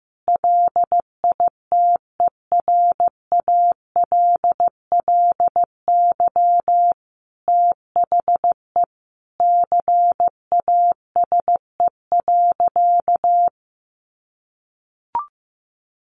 Morse code, heard as a keyed tone, about 15 wpm, 700 Hz. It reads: LITERALLY THE CASE.
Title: Sentences from Top 1400 Words 15wpm